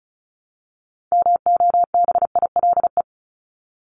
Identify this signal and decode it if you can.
Morse code, heard as a keyed tone, about 34 wpm, 700 Hz. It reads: MOBILE
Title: Top 1751-2000 Words 35wpm